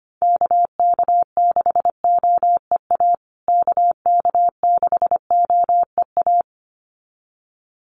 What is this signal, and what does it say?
Morse code, heard as a keyed tone, about 25 wpm, 700 Hz. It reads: KK6OEA KK6OEA
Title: USA Call Signs Repeated 2x 25wpm